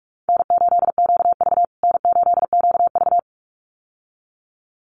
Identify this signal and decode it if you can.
Morse code, heard as a keyed tone, about 44 wpm, 700 Hz. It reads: N8QV N8QV
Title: USA Call Signs Repeated 2x 45wpm